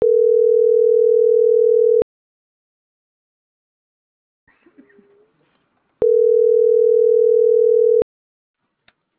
Ringback tone (North America), 440 + 480 Hz.